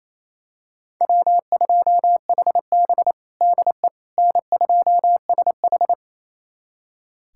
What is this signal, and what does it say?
Morse code, heard as a keyed tone, about 28 wpm, 700 Hz. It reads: W2HB DE N2SH
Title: USA Callsign DE Callsign - 4 Characters - 28wpm